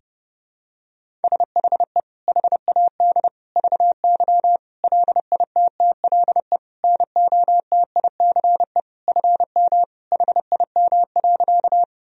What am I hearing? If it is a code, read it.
Morse code, heard as a keyed tone, about 30 wpm, 700 Hz. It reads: SHE HAD VY LITTLE NOTICE FM HIM.
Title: Sentences from Top 400 Words and Abbreviations 30wpm